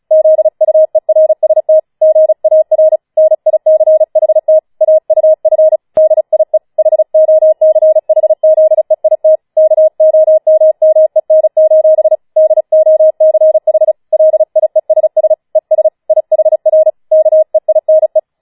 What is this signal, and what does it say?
Morse code, heard as a keyed tone, about 36 wpm, 630 Hz. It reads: ZUERST GAR NICHT AUF DIE HOCHZEIT KOMMEN: DOCH LIESS ES IHR KEINE
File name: Schneewittchen Teil6 36wpm